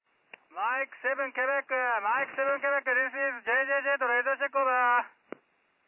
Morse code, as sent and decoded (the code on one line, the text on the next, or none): none